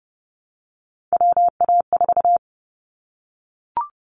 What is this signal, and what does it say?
Morse code, heard as a keyed tone, about 30 wpm, 700 Hz. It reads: WA4
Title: Callsign Prefixes for Top 25 DXCC - Repeated 3x - 30wpm